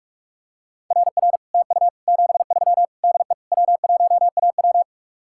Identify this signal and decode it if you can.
Morse code, heard as a keyed tone, about 48 wpm, 700 Hz. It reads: RR TU 73 DE W1AW